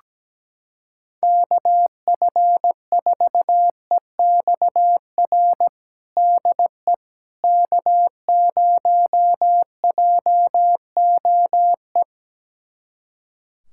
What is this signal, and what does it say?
Morse code, heard as a keyed tone, about 17 wpm, 700 Hz. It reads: KF4EXR DE K0JOE
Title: USA Callsign DE Callsign 17wpm